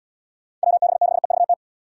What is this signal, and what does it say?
Morse code, heard as a keyed tone, about 67 wpm, 700 Hz. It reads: WB6EVT